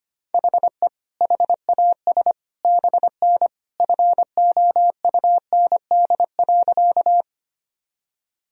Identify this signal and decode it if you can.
Morse code, heard as a keyed tone, about 25 wpm, 700 Hz. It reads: HE HAS BN FOUND.
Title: Sentences from Top 300 Words and Abbreviations 25wpm